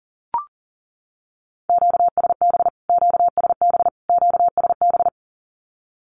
Morse code, heard as a keyed tone, about 40 wpm, 700 Hz. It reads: QSB QSB QSB
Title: Q-Codes Repeated 3x 40wpm